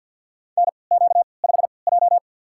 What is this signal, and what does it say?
Morse code, heard as a keyed tone, about 51 wpm, spaced as for 29 wpm, 700 Hz. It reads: NQ5J
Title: USA Call Signs - 3x Character Spacing - 50wpm